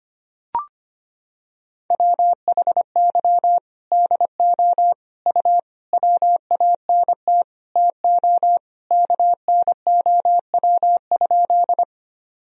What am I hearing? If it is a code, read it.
Morse code, heard as a keyed tone, about 25 wpm, 700 Hz. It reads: WHY DO U WANT TO KNOW?
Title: Sentences from Top 200 Words and Abbreviations 25wpm